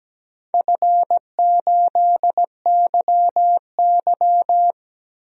Morse code, heard as a keyed tone, about 17 wpm, 700 Hz. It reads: F8YY
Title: DXCC-100 Call Signs - 4 Characters - 17wpm